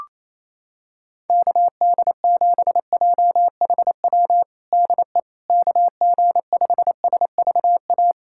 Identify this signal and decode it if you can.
Morse code, heard as a keyed tone, about 28 wpm, 700 Hz. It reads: KD7JHW DE KG5SVA
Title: USA Callsign DE Callsign - 6 Characters - 28wpm